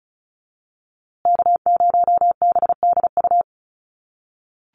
Morse code, heard as a keyed tone, about 34 wpm, 700 Hz. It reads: K0BDU
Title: USA Call Signs Repeated 3x 35wpm